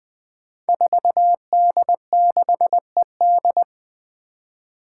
Morse code, heard as a keyed tone, about 20 wpm, 700 Hz. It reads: 4D6ED
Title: Five Challenging Letter-Number Combinations 20wpm